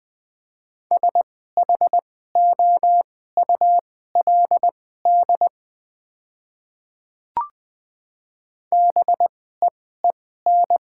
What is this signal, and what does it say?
Morse code, heard as a keyed tone, about 20 wpm, spaced as for 14 wpm, 690 Hz. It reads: SHOULD BEEN
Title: 2x Character Spacing - Top 500 Words - 20wpm